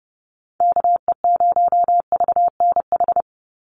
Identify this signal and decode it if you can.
Morse code, heard as a keyed tone, about 29 wpm, 700 Hz. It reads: KE0VNH
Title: USA Call Signs 30wpm